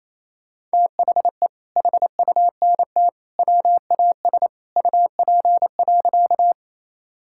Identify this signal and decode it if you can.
Morse code, heard as a keyed tone, about 27 wpm, 700 Hz. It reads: THE HUNT WAS UP.
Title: Sentences from Top 1600 Words 28wpm